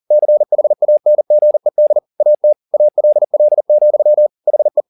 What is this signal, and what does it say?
Morse code, heard as a keyed tone, about 40 wpm, 600 Hz. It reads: CHANGED AT ALL, HE
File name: the old man and the sea 2 40wpm